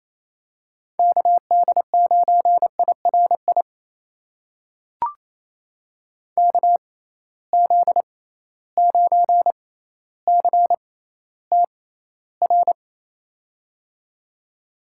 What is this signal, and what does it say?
Morse code, heard as a keyed tone, about 27 wpm, spaced as for 14 wpm, 700 Hz. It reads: KD9IRI KZ9CTR
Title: USA Call Signs - 6 Characters - 6x Character Spacing - 28wpm